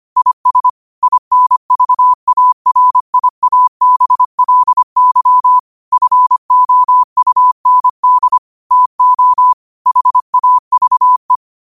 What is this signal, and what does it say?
Morse code, heard as a keyed tone, about 25 wpm, 1000 Hz. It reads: IS INVARIABLY FOUND TO HAVE
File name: qotdfiles / 25wpm